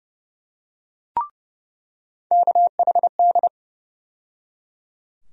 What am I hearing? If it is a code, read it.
Morse code, heard as a keyed tone, about 30 wpm, 700 Hz. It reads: KHD
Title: USA Call Sign Postfixes - 3 Letters 30wpm